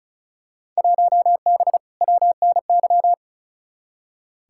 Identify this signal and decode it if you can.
Morse code, heard as a keyed tone, about 34 wpm, 700 Hz. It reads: 1B WNY
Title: Field Day Exchanges Encore - 35wpm